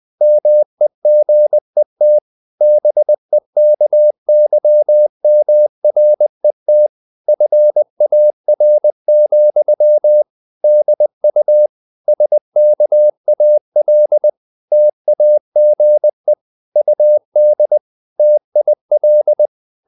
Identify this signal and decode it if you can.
Morse code, heard as a keyed tone, about 20 wpm, 600 Hz. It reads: MEGET BEKYMRET FAR, DU SKAL TAGE UD TIL
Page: Ferie 20 wpm